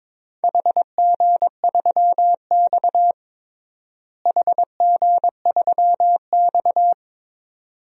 Morse code, heard as a keyed tone, about 23 wpm, 700 Hz. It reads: HG3X HG3X
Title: DXCC-100 Call Signs - 4 Characters - Repeated 2x - 22wpm